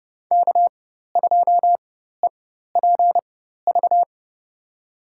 Morse code, heard as a keyed tone, about 32 wpm, spaced as for 14 wpm, 700 Hz. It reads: K2EPV
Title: USA Call Signs - 5 Characters - 4x Character Spacing - 30wpm